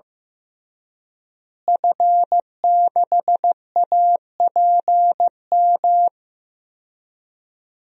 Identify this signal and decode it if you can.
Morse code, heard as a keyed tone, about 15 wpm, 700 Hz. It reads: F6APM
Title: DXCC-100 Call Signs - 5 Characters - Repeated 3x - 15wpm